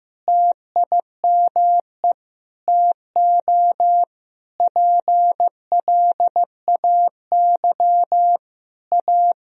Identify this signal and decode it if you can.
Morse code, heard as a keyed tone, about 15 wpm, 700 Hz. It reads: TIME TO PLAY A
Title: Sentences from Top 1000 Words - Review Part 4 - 15wpm